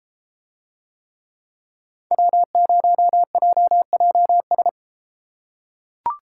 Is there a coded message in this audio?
Yes: Morse code from 2.1 to 4.7 s, heard as a keyed tone, about 33 wpm, 700 Hz. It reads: W0JJS